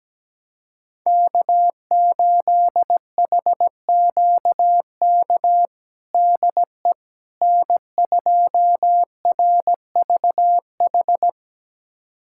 Morse code, heard as a keyed tone, about 17 wpm, 700 Hz. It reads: K8HQK DE N2RVH
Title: USA Callsign DE Callsign - 5 Characters - 17wpm